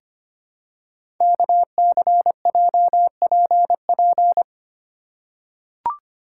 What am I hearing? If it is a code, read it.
Morse code, heard as a keyed tone, about 25 wpm, 700 Hz. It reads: KCJPP
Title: Code Groups - 5 Letters - 25wpm